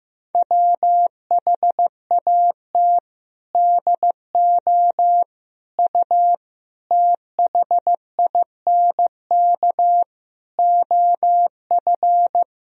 Morse code, heard as a keyed tone, about 15 wpm, 700 Hz. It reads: WHAT DO U THINK OF
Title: Sentences from Top 200 Words and Abbreviations 15wpm